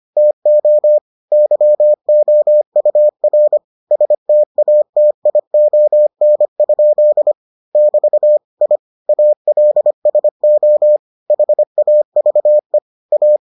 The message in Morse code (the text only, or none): TO YOUR STATION? <BT> I ALSO HAVE A